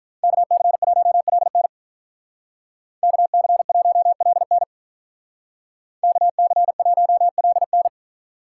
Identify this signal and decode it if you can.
Morse code, heard as a keyed tone, about 45 wpm, 700 Hz. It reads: KC1LN KC1LN KC1LN